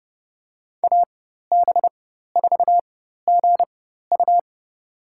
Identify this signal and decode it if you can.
Morse code, heard as a keyed tone, about 30 wpm, spaced as for 14 wpm, 700 Hz. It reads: AB4GU
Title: USA Call Signs - 4x Character Spacing - 30wpm